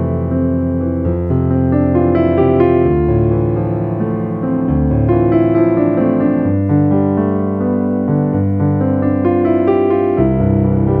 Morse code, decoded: none